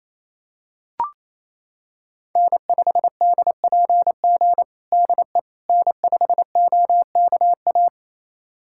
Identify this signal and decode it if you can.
Morse code, heard as a keyed tone, about 28 wpm, 700 Hz. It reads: N5DPG DE N5OKA
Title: USA Callsign DE Callsign - 5 Characters - 28wpm